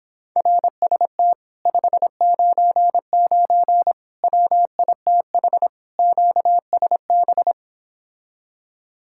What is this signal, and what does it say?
Morse code, heard as a keyed tone, about 26 wpm, 700 Hz. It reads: RST 599 WITH QSB